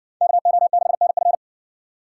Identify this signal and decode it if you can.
Morse code, heard as a keyed tone, about 54 wpm, 700 Hz. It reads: KO6NF